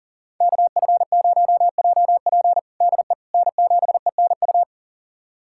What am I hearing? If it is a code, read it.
Morse code, heard as a keyed tone, about 40 wpm, 700 Hz. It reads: KF0JP DE N7ENU
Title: USA Callsign DE Callsign - 5 Characters - 40wpm